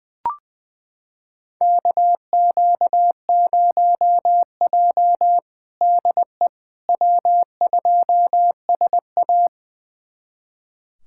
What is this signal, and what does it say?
Morse code, heard as a keyed tone, about 20 wpm, 700 Hz. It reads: KQ0J DE W2SA
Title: USA Callsign DE Callsign - 4 Characters - 20wpm